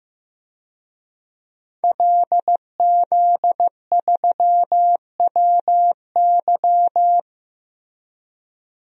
Morse code, heard as a keyed tone, about 15 wpm, 700 Hz. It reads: LZ3WY
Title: DXCC-100 Call Signs - 5 Characters - 15wpm